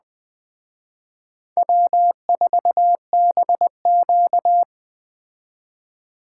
Morse code, heard as a keyed tone, about 20 wpm, 700 Hz. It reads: W4BQ
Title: Speed Racing - USA 1x2 Call Signs 17wpm